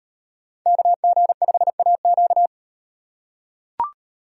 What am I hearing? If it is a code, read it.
Morse code, heard as a keyed tone, about 38 wpm, 700 Hz. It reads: KG5AQ